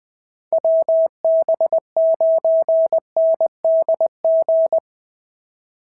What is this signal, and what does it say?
Morse code, heard as a keyed tone, about 19 wpm, 650 Hz. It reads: WB9NDG
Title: POTA QSO Practice - 20wpm